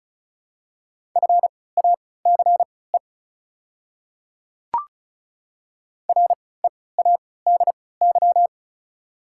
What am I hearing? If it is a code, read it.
Morse code, heard as a keyed tone, about 36 wpm, spaced as for 20 wpm, 700 Hz. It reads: FACE READY
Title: 3x Character Spacing - Top 500 Words - 35wpm